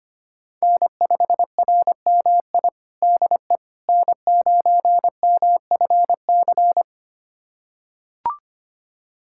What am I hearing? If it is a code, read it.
Morse code, heard as a keyed tone, about 25 wpm, 700 Hz. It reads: N5RMI DE N9MFC
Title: USA Callsign DE Callsign - 5 Characters - 25wpm